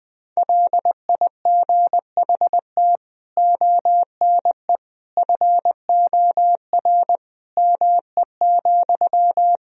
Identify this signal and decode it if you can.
Morse code, heard as a keyed tone, about 20 wpm, 700 Hz. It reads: LIGHT ONE FOR ME,
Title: Sentences from Top 1000 Words - Review Part 1 - 20wpm